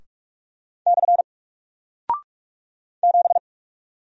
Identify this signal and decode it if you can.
Morse code, heard as a keyed tone, about 44 wpm, 700 Hz. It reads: / 7
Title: Single Character (Letter:Number:Punctuation:Prosign) Alt 45wpm